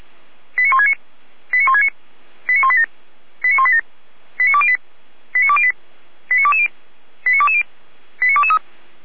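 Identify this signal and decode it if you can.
Five-tone selective call (ZVEI), heard as quick runs of five tones, tones of about 70 ms. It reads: 89169 89169 89187 89187 89208 89208 89220 89220 89223
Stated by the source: ZVEI five-tone selective call 89174 received on SDR.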